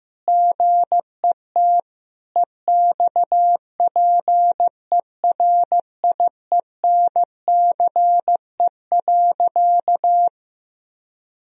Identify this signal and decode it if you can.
Morse code, heard as a keyed tone, about 16 wpm, 700 Hz. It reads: GET EXPERIENCE.
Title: Sentences from Top 1000 Words - Review Part 3 - 15wpm